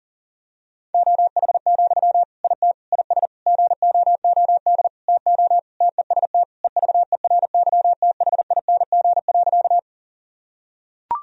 Morse code, heard as a keyed tone, about 40 wpm, 700 Hz. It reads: OH, IT IS GOOD TO TEST EVERYTHING.
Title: Sentences from Top 1000 Words - Review Part 3 - 40wpm